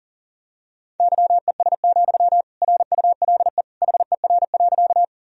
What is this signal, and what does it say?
Morse code, heard as a keyed tone, about 40 wpm, 700 Hz. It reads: YES, RULE HER.
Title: Sentences from Top 1000 Words - Review Part 2 - 40wpm